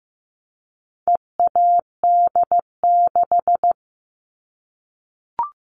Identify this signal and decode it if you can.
Morse code, heard as a keyed tone, about 15 wpm, 700 Hz. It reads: EAD6
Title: All Cut Numbers - 4-Digit 15wpm